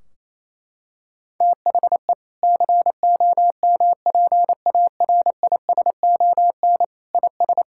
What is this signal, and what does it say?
Morse code, heard as a keyed tone, about 27 wpm, 700 Hz. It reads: THE COMPARISON IS
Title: Sentences from Top 1600 Words 28wpm